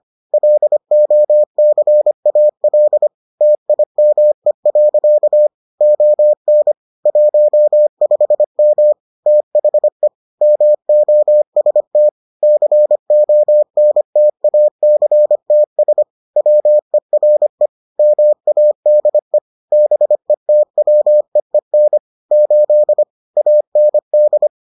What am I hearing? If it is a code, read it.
Morse code, heard as a keyed tone, about 25 wpm, 600 Hz. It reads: LOCAL TIME. ON 15M THE MOST CONTACTS WERE MADE BETWEEN 8 AND